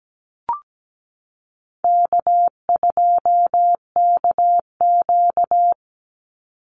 Morse code, heard as a keyed tone, about 17 wpm, 700 Hz. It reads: K2KQ
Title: Easy Call Signs - 17wpm